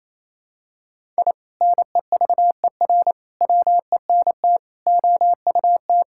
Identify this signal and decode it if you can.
Morse code, heard as a keyed tone, about 28 wpm, 700 Hz. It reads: I NEVER WENT OUT
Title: Sentences from Top 1000 Words - Review Part 5 - 28wpm